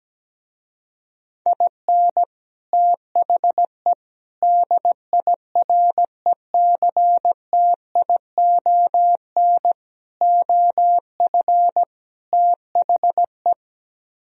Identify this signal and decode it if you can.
Morse code, heard as a keyed tone, about 17 wpm, 700 Hz. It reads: IN THE DIRECTION OF THE
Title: Speed-Racing - Sets of 5 Words 17wpm